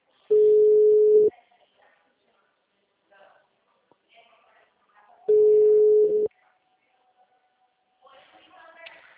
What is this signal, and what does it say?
Ringback tone (Europe, 425 Hz), with speech.